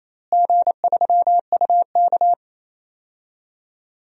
Morse code, heard as a keyed tone, about 28 wpm, 700 Hz. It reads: G3UK
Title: DXCC-100 Call Signs - 4 Characters - Repeated 3x - 28wpm